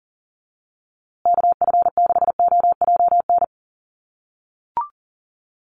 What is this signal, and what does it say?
Morse code, heard as a keyed tone, about 40 wpm, 700 Hz. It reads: KF6OJN
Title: USA Call Signs Repeated 2x 40wpm